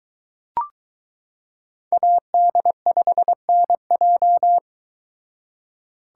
Morse code, heard as a keyed tone, about 22 wpm, 700 Hz. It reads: AD5NJ